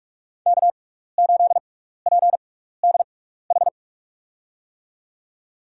Morse code, heard as a keyed tone, about 46 wpm, spaced as for 16 wpm, 700 Hz. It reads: K8PDH
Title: USA Call Signs - 5 Characters - 6x Character Spacing - 45wpm